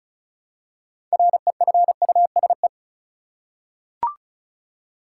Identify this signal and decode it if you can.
Morse code, heard as a keyed tone, about 34 wpm, 700 Hz. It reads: REFUSE
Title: Six Letter Words - 35wpm